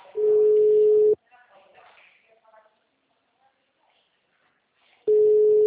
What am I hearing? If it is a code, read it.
Ringback tone (Europe, 425 Hz), with speech.